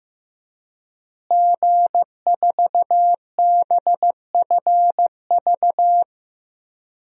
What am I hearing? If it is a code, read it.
Morse code, heard as a keyed tone, about 15 wpm, 700 Hz. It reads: G4BFV
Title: DXCC-100 Call Signs - 5 Characters - Repeated 2x - 15wpm